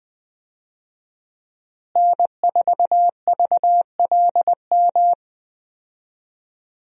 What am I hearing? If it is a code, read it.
Morse code, heard as a keyed tone, about 21 wpm, 700 Hz. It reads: N4VLM
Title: USA Call Signs - 5 Characters - Repeated 2x - 20wpm